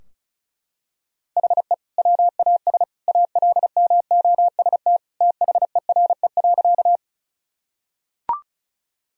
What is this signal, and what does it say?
Morse code, heard as a keyed tone, about 34 wpm, 700 Hz. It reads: HE WAS ALMOST THERE.
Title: Sentences from Top 1000 Words - Review Part 4 - 35wpm